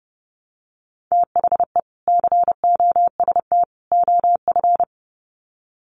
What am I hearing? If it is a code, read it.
Morse code, heard as a keyed tone, about 30 wpm, 700 Hz. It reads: THE COST OF
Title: Sets of 3 Words 30wpm